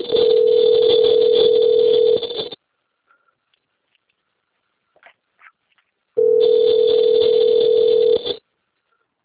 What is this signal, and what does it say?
Ringback tone (North America), 440 + 480 Hz.
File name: rg-1600-unknown- - - .5250